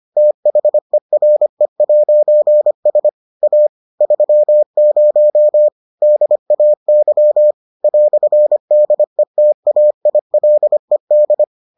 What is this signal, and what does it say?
Morse code, heard as a keyed tone, about 25 wpm, 600 Hz. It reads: THERE'S A 30 DAY "DETAILED